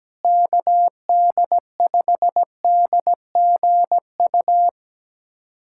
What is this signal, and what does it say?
Morse code, heard as a keyed tone, about 17 wpm, 700 Hz. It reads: KD5DGU
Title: USA Callsign DE Callsign 17wpm